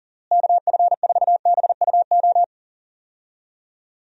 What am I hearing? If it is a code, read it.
Morse code, heard as a keyed tone, about 37 wpm, 700 Hz. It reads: KF4BUO
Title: USA Call Signs - 6 Characters - Repeated 3x - 40wpm